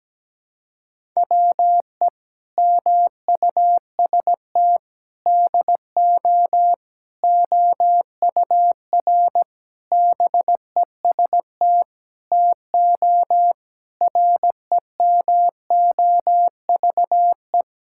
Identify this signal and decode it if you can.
Morse code, heard as a keyed tone, about 17 wpm, 700 Hz. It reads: WE MUST DO OUR BEST TO REMOVE
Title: Sentences from Top 1000 Words - Review Part 5 - 17wpm